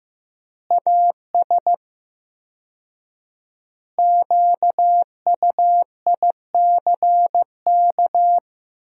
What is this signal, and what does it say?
Morse code, heard as a keyed tone, about 16 wpm, 700 Hz. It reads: AS QUICK
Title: 4x Word-Spacing - Sentences from Top 500 Words - 15wpm